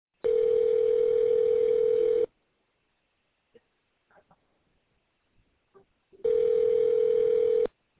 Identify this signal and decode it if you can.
Ringback tone (North America), 440 + 480 Hz.